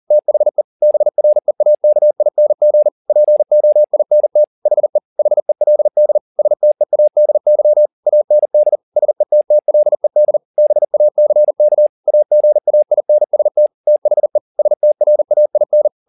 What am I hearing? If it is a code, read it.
Morse code, heard as a keyed tone, about 40 wpm, 600 Hz. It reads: THE BREAKING POINT HE HELD STEADY AND SETTLED BACK AGAINST THE STRAIN